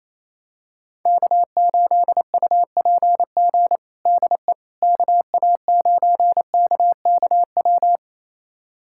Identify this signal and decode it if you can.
Morse code, heard as a keyed tone, about 28 wpm, 700 Hz. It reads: K8UPG DE KA9KKW
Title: USA Callsign DE Callsign 28wpm